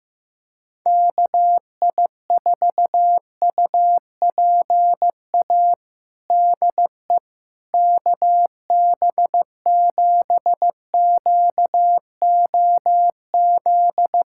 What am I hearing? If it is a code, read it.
Morse code, heard as a keyed tone, about 16 wpm, 700 Hz. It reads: KI4UPA DE KB7QOZ
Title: USA Callsign DE Callsign 15wpm